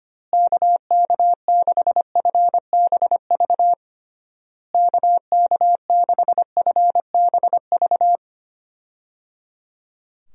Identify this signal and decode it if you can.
Morse code, heard as a keyed tone, about 25 wpm, 700 Hz. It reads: KK6FBV KK6FBV
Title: USA Call Signs - 6 Characters - Repeated 3x - 25wpm